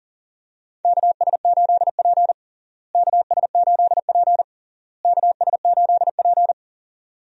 Morse code, heard as a keyed tone, about 40 wpm, 700 Hz. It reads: KS8P KS8P KS8P
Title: USA Call Signs - 4 Characters - Repeated 3x - 40wpm